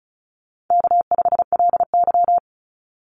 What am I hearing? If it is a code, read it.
Morse code, heard as a keyed tone, about 36 wpm, 700 Hz. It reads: K5LY
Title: Call Signs - 35wpm